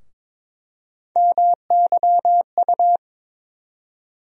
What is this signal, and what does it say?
Morse code, heard as a keyed tone, about 22 wpm, 700 Hz. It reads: MYU
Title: USA Call Sign Postfixes - 3 Letters 22wpm